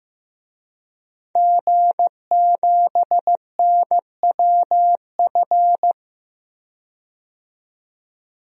Morse code, heard as a keyed tone, about 15 wpm, 700 Hz. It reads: G7NWF
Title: DXCC-100 Call Signs - 5 Characters - 15wpm